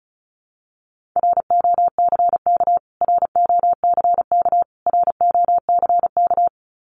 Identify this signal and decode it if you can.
Morse code, heard as a keyed tone, about 34 wpm, 700 Hz. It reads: ROCK ROCK ROCK
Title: Top 500 Words Repeated 35wpm